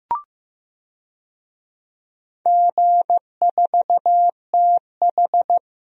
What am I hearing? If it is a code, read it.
Morse code, heard as a keyed tone, about 15 wpm, 700 Hz. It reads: G4TH
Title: DXCC-100 Call Signs - 4 Characters - 15wpm